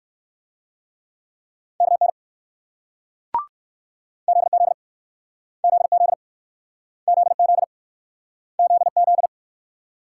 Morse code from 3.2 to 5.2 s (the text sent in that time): ZZ